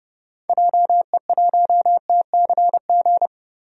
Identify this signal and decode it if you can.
Morse code, heard as a keyed tone, about 30 wpm, 700 Hz. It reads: JE1TCG
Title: DXCC-100 Call Signs - 6 Characters - Repeated 3x - 30wpm